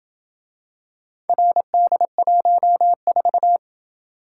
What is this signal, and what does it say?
Morse code, heard as a keyed tone, about 28 wpm, 700 Hz. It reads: RD14